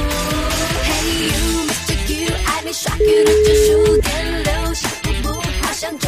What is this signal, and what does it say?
Ringback tone (Europe, 425 Hz), with noise.